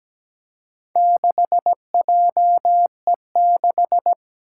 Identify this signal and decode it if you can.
Morse code, heard as a keyed tone, about 17 wpm, 700 Hz. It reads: 6JE6
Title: Four Challenging Letter-Number Combinations 17wpm